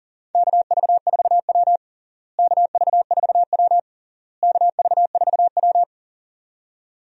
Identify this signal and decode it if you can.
Morse code, heard as a keyed tone, about 37 wpm, 700 Hz. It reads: KV4W KV4W KV4W
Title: USA Call Signs - 4 Characters - Repeated 3x - 40wpm